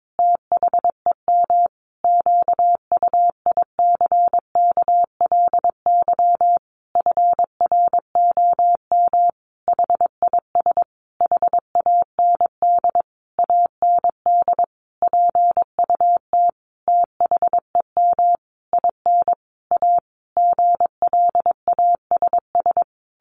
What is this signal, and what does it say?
Morse code, heard as a keyed tone, about 22 wpm, 700 Hz. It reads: THEM QUICKLY FROM HIS HAND AND PUT THEM IN A GLASS
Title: Sentences from Top 1300 Words 22wpm